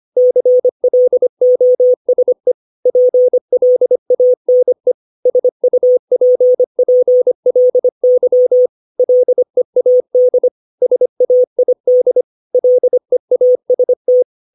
Morse code, heard as a keyed tone, about 25 wpm, 500 Hz. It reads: CLOSE PLANE SUPPLY LEAD SAID LEAST
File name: 632 words / 632 Words 25WPM